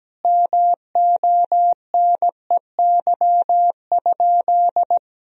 Morse code, heard as a keyed tone, about 17 wpm, 700 Hz. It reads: MONEY?
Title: Sentences from Top 400 Words and Abbreviations 17wpm